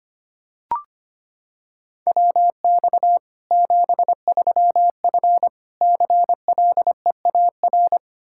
Morse code, heard as a keyed tone, about 25 wpm, 700 Hz. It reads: WX 73F CLEAR
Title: Practice QSO - Exchange 2 - Info - 25wpm